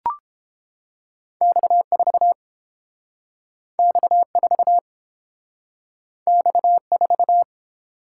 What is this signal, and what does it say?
Morse code, heard as a keyed tone, about 30 wpm, 700 Hz. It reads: X4 X4 X4